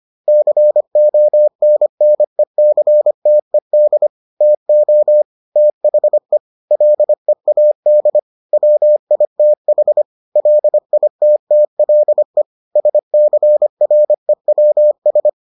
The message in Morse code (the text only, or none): CONNECTED TO THE LEAD WITH LITTLE SCREWS